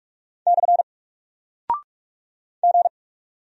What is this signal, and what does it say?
Morse code, heard as a keyed tone, about 44 wpm, 700 Hz. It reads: / G
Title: Single Character (Letter:Number:Punctuation:Prosign) 45wpm